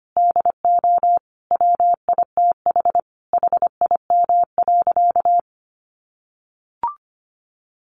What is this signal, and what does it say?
Morse code, heard as a keyed tone, about 25 wpm, 700 Hz. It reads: DO WITH HIM.
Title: Sentences from Top 1000 Words 25wpm